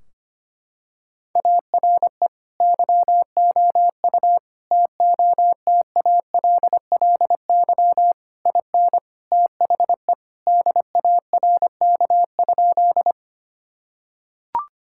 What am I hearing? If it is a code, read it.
Morse code, heard as a keyed tone, about 25 wpm, 700 Hz. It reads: ARE YOU TOTALLY IN THE DARK?
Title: Sentences from Top 1600 Words 25wpm